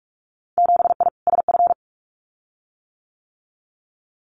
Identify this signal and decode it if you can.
Morse code, heard as a keyed tone, about 46 wpm, 700 Hz. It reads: 7I SF
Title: Winter Field Day Exchanges Encore 45wpm